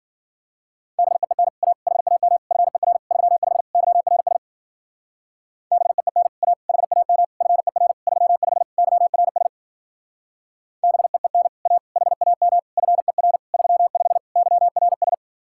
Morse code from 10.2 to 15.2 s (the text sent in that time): BEEN A HAM FER 35 YRS